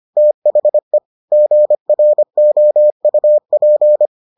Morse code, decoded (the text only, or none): THE GROUP